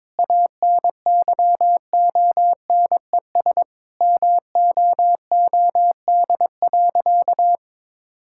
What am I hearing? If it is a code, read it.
Morse code, heard as a keyed tone, about 22 wpm, 700 Hz. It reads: ANYONES MOOD.
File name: Difficult Sentences with Hints - 22wpm